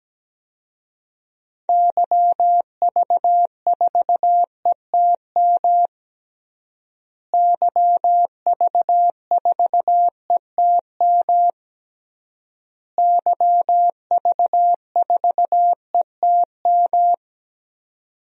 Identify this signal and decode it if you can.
Morse code, heard as a keyed tone, about 17 wpm, 700 Hz. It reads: YV4ETM YV4ETM YV4ETM
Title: DXCC-100 Call Signs - 6 Characters - Repeated 3x - 17wpm